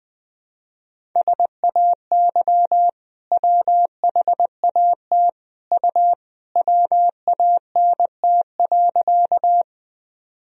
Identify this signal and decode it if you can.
Morse code, heard as a keyed tone, about 20 wpm, 700 Hz. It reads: SAY WHAT U WANT.
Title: Sentences from Top 200 Words and Abbreviations 20wpm